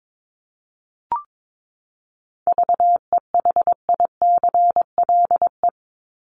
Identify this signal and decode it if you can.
Morse code, heard as a keyed tone, about 22 wpm, 700 Hz. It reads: VEHICLE
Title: Top 1751-2000 Words 22wpm